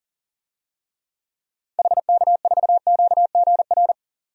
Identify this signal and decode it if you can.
Morse code, heard as a keyed tone, about 40 wpm, 700 Hz. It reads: HK4QGR
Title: DXCC-100 Call Signs - 6 Characters - 40wpm